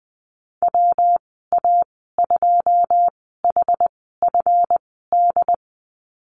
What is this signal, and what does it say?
Morse code, heard as a keyed tone, about 20 wpm, spaced as for 14 wpm, 700 Hz. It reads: WA2HFD
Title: 193 Lesson 31g - Letter v - Call Signs 20wpm